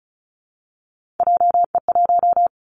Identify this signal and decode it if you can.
Morse code, heard as a keyed tone, about 34 wpm, 700 Hz. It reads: JE1
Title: Callsign Prefixes for Top 25 DXCC - 35wpm